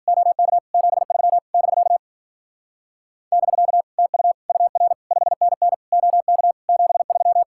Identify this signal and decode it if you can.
Morse code, heard as a keyed tone, about 51 wpm, 700 Hz. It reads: OK 73 <BK> <BK> TU UR 5NN OK 73